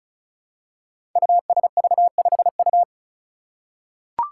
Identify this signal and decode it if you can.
Morse code, heard as a keyed tone, about 36 wpm, 700 Hz. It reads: USV5U
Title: Five Challenging Letter-Number Combinations 35wpm